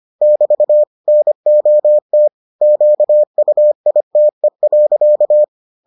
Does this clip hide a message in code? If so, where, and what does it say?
Yes: Morse code from 0.2 to 5.4 s, heard as a keyed tone, about 25 wpm, 600 Hz. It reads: <BT> NOT QUITE.